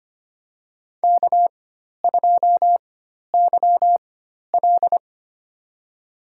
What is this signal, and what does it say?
Morse code, heard as a keyed tone, about 25 wpm, spaced as for 12 wpm, 700 Hz. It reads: K2YL
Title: USA Call Signs - 4 Characters - 4x Character Spacing - 25wpm